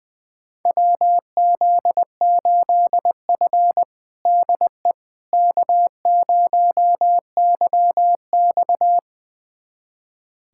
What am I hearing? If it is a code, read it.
Morse code, heard as a keyed tone, about 21 wpm, 700 Hz. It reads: WZ8F DE K0YX
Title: USA Callsign DE Callsign - 4 Characters - 20wpm